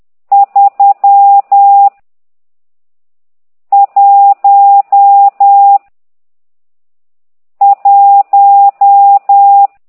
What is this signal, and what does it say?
Morse code, heard as a keyed tone, about 10 wpm, 800 Hz. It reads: 3 1 1
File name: ct2idw / exerc12